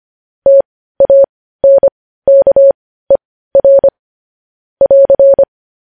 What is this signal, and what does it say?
Morse code, heard as a keyed tone, about 25 wpm, spaced as for 15 wpm, 550 Hz. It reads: TANKER <AR>
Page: news headlines in morse code at 15 wpm